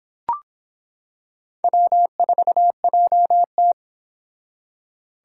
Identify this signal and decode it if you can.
Morse code, heard as a keyed tone, about 26 wpm, 700 Hz. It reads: W4JT